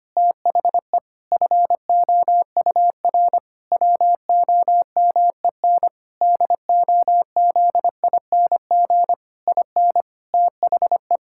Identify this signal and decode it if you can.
Morse code, heard as a keyed tone, about 25 wpm, 700 Hz. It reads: THE FOUR WOMEN DOZING IN THE
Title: Pangram Challenge v2 - 25wpm